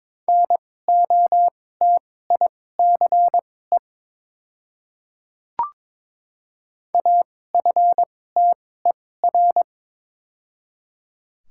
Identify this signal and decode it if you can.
Morse code, heard as a keyed tone, about 22 wpm, spaced as for 16 wpm, 690 Hz. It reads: NOTICE AFTER
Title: 2x Character Spacing - Top 500 Words - 22wpm